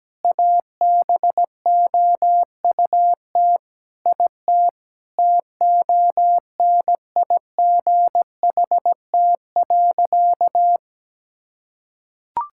Morse code, heard as a keyed tone, about 17 wpm, 700 Hz. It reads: ABOUT IT TONIGHT.
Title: Sentences from Top 1000 Words - Review Part 5 - 17wpm